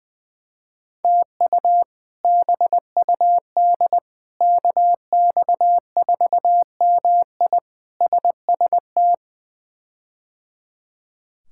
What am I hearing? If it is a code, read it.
Morse code, heard as a keyed tone, about 21 wpm, 700 Hz. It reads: TU BUD KX4MI SST
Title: K1USN SST QSO Practice - 20wpm